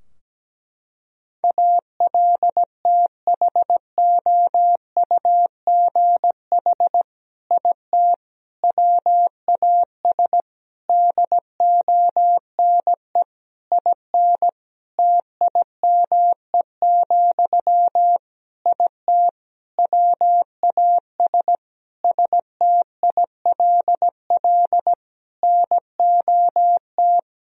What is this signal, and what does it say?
Morse code, heard as a keyed tone, about 17 wpm, 700 Hz. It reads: ALTHOUGH IT WAS DONE IN TIME, IT WAS STILL NOT
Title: Sentences from Top 1000 Words - Review Part 3 - 17wpm